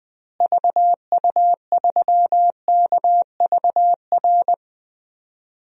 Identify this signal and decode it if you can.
Morse code, heard as a keyed tone, about 20 wpm, 700 Hz. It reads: VU3KVR
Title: DXCC-100 Call Signs - 6 Characters - Repeated 3x - 20wpm